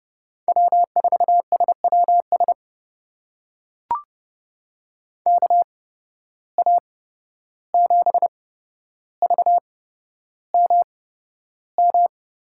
Morse code, heard as a keyed tone, about 30 wpm, spaced as for 12 wpm, 700 Hz. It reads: W4SWS KA7VMM
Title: USA Call Signs - 8x Character Spacing - 30wpm